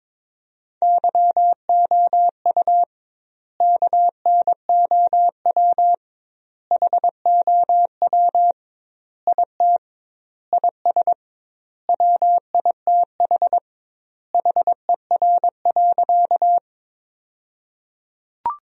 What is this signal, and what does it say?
Morse code, heard as a keyed tone, about 22 wpm, 700 Hz. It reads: YOU KNOW HOW IT IS WITH HER.
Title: 2x Word-Spacing - Sentences from Top 100 Words - 22wpm